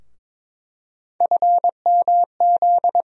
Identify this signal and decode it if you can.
Morse code, heard as a keyed tone, about 22 wpm, 700 Hz. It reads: FMZ
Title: USA Call Sign Postfixes - 3 Letters 22wpm